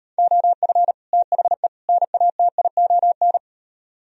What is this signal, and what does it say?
Morse code, heard as a keyed tone, about 38 wpm, 700 Hz. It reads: OF THE NATION